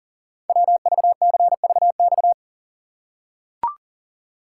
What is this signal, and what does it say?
Morse code, heard as a keyed tone, about 39 wpm, 690 Hz. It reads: WVCVX
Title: Code Groups - 5 Letter:Numbers - 40wpm